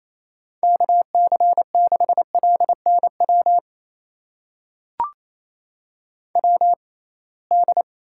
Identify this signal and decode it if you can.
Morse code, heard as a keyed tone, about 28 wpm, spaced as for 21 wpm, 700 Hz. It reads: KC6LNW WD
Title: USA Call Signs - 6 Characters - 6x Character Spacing - 28wpm